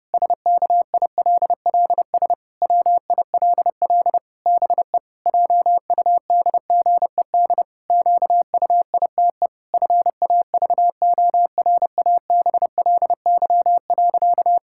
Morse code, heard as a keyed tone, about 32 wpm, 700 Hz. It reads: SKILLS WILL BE JUDGED QUITE FAVORABLY.
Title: Pangram Challenge v2 - 30wpm